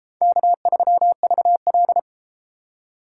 Morse code, heard as a keyed tone, about 33 wpm, 700 Hz. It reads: K3VL